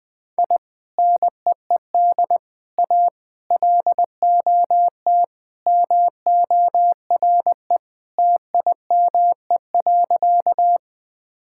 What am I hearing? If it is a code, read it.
Morse code, heard as a keyed tone, about 20 wpm, 700 Hz. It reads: I NEED A LOT MORE TIME.
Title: Sentences with 3-Word Phrases - 20wpm